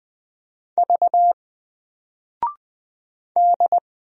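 Morse code, heard as a keyed tone, about 20 wpm, 700 Hz. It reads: V D
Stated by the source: Single letter and number Morse drill at 20 wpm, from N4LCD.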